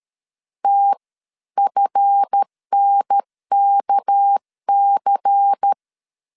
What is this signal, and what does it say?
Morse code, heard as a keyed tone, about 13 wpm, spaced as for 11 wpm, 800 Hz. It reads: TFNKC
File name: LEZIONI / 7 GF9Z / 7 GF 12wpm 2122